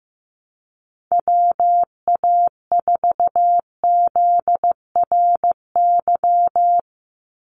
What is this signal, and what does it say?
Morse code, heard as a keyed tone, about 15 wpm, 700 Hz. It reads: WA4ZRY
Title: USA Call Signs 15wpm